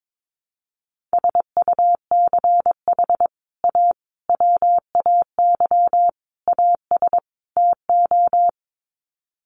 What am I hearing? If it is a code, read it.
Morse code, heard as a keyed tone, about 22 wpm, 700 Hz. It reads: SUCH A WAY AS TO
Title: Sets of 5 Words Encore2 22wpm